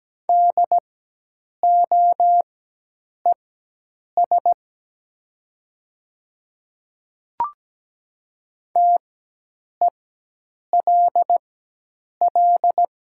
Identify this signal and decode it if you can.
Morse code, heard as a keyed tone, about 17 wpm, spaced as for 8 wpm, 700 Hz. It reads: DOES TELL
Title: 4x Character Spacing - Top 200 Words - 17wpm